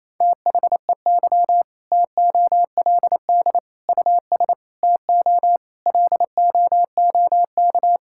Morse code, heard as a keyed tone, about 28 wpm, 700 Hz. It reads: THEY TOLD US TO LOOK
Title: Sentences from Top 1600 Words 28wpm